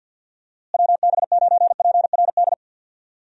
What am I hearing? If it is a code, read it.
Morse code, heard as a keyed tone, about 48 wpm, 700 Hz. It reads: WB9PRD
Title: USA Call Signs - 6 Characters - Repeated 2x - 50wpm